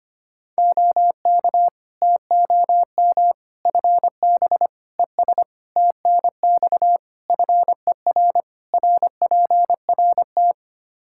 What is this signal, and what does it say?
Morse code, heard as a keyed tone, about 25 wpm, 700 Hz. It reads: OK TOM FB ES TNX FER RPRT
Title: Practice QSO - Exchange 2 - Info - 25wpm